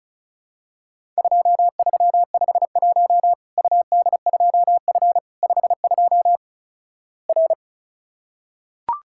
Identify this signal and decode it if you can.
Morse code, heard as a keyed tone, about 36 wpm, 700 Hz. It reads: 2351 UD2F 52 R
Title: Worked All Europe DX Contest - QTC Exchange - 35wpm